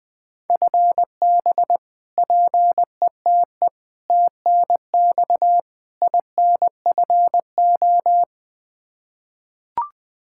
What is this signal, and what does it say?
Morse code, heard as a keyed tone, about 20 wpm, 700 Hz. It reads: FB PETE TNX INFO
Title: Practice QSO - Exchange 3 - More Info - 20wpm